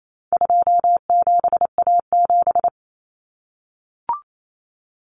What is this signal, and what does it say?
Morse code, heard as a keyed tone, about 28 wpm, 700 Hz. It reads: 27A7
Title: Common Cut Numbers - 4-Digit 28wpm